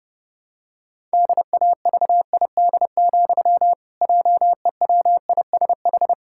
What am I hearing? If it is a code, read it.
Morse code, heard as a keyed tone, about 30 wpm, 700 Hz. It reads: DAVID, JEWISH
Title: Pangram Challenge v2 - 30wpm